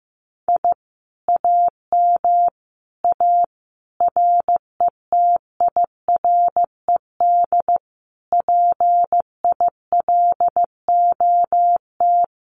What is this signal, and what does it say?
Morse code, heard as a keyed tone, about 15 wpm, 700 Hz. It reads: I AM A RETIRED PILOT
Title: Common QSO Phrases Encore 15wpm